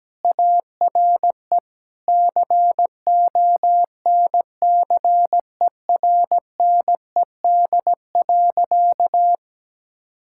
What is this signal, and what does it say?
Morse code, heard as a keyed tone, about 17 wpm, 700 Hz. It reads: ARE CONCERNED.
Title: Sentences from Top 1000 Words - Review Part 6 - 17wpm